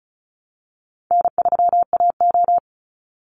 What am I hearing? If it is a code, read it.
Morse code, heard as a keyed tone, about 34 wpm, 700 Hz. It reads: N3AO
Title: Easy Call Signs Repeated 3x - 35wpm